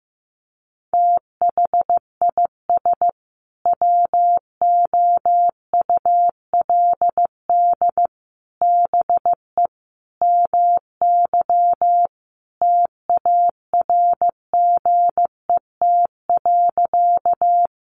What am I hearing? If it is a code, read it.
Morse code, heard as a keyed tone, about 15 wpm, 700 Hz. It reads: THIS WOULD BE MY TARGET.
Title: Sentences from Top 1200 Words 15wpm